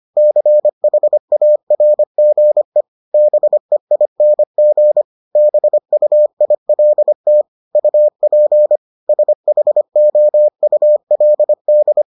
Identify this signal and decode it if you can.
Morse code, heard as a keyed tone, about 25 wpm, 600 Hz. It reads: CHARGE BEING BUILT UP SHOULD